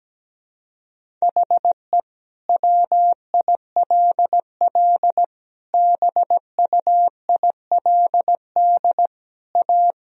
Morse code, heard as a keyed tone, about 17 wpm, 700 Hz. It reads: HE WILL BUILD A
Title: Sentences from Top 1000 Words - Review Part 1 - 17wpm